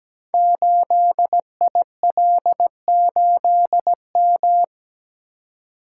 Morse code, heard as a keyed tone, about 17 wpm, 700 Hz. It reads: 8IL8M
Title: Code Groups - 5 Letter:Numbers - 17wpm